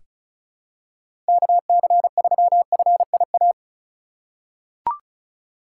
Morse code, heard as a keyed tone, about 36 wpm, 700 Hz. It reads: KC3FIA
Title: USA Call Signs - 2x Character Spacing - 35wpm